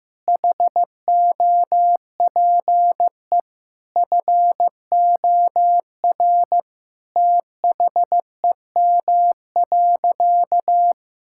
Morse code, heard as a keyed tone, about 16 wpm, 700 Hz. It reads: HOPE FOR THEM.
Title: Sentences from Top 1000 Words - Review Part 5 - 15wpm